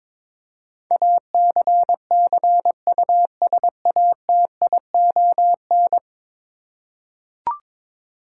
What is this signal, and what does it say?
Morse code, heard as a keyed tone, about 21 wpm, 700 Hz. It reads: ACCUSATION
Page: morse-code-ninja-practice-22wpm